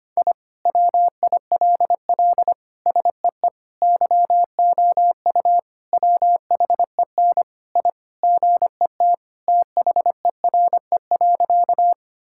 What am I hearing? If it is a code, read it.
Morse code, heard as a keyed tone, about 25 wpm, 700 Hz. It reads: I WILL SEE YOU WHEN I GET THERE.
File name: practice / 25wpm / Sentences with 2-Word Phrases - 25wpm